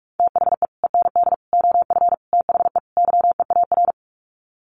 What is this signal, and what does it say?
Morse code, heard as a keyed tone, about 44 wpm, 700 Hz. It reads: THE END OF THE YEAR
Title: Sets of 5 Words 45wpm